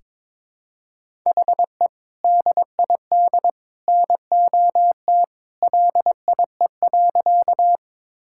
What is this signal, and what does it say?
Morse code, heard as a keyed tone, about 21 wpm, 700 Hz. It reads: HE DID NOT LIE.
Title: Sentences from Top 1600 Words 22wpm